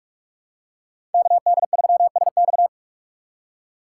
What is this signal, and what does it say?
Morse code, heard as a keyed tone, about 45 wpm, 700 Hz. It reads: KD3SX
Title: USA Call Signs - 5 Characters - 4x Character Spacing - 45wpm